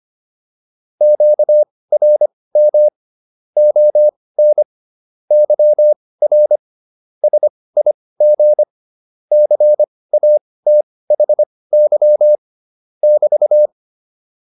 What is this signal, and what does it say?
Morse code, heard as a keyed tone, about 25 wpm, spaced as for 18 wpm, 600 Hz. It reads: QRM ON YR SIG CATHY <BT>
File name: QSO308 18